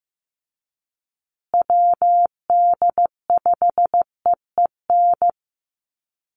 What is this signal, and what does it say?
Morse code, heard as a keyed tone, about 15 wpm, 700 Hz. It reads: WD5EEN
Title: USA Call Signs Repeated 3x 15wpm